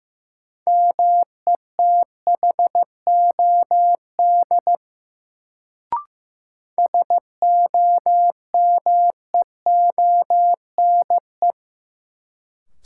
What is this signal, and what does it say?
Morse code, heard as a keyed tone, about 16 wpm, 700 Hz. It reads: METHOD SOMEONE
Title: Sentences from Top 600 Words - New Word Drill - 15wpm